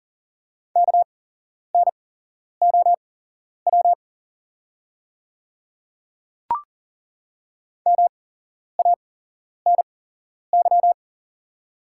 Morse code, heard as a keyed tone, about 40 wpm, spaced as for 11 wpm, 690 Hz. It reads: KNOW MANY
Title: 8x Character Spacing - Top 100 Words - 40wpm